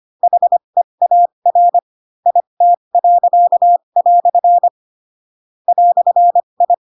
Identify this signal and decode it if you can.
Morse code, heard as a keyed tone, about 25 wpm, spaced as for 22 wpm, 700 Hz. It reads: HEAR IT." "I
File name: Pinocchio / Pinocchio ch2 18 25 ik1hge